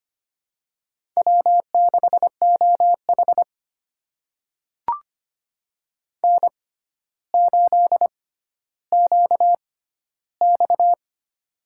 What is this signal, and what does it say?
Morse code, heard as a keyed tone, about 25 wpm, spaced as for 13 wpm, 700 Hz. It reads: W6OH N8QX
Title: USA Call Signs - 4 Characters - 6x Character Spacing - 25wpm